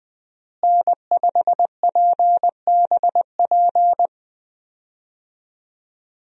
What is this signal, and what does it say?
Morse code, heard as a keyed tone, about 20 wpm, 700 Hz. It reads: N5PBP
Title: USA Call Signs - 5 Characters - Repeated 3x - 20wpm